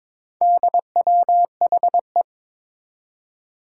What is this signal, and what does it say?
Morse code, heard as a keyed tone, about 22 wpm, 700 Hz. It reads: DWHE
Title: Four Challenging Letter-Number Combinations 22wpm